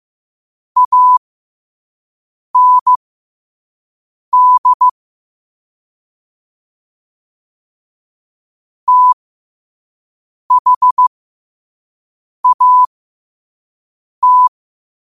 Morse code, heard as a keyed tone, about 15 wpm, spaced as for 5 wpm, 1000 Hz. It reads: AND THAT
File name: qotdfiles / 5wpm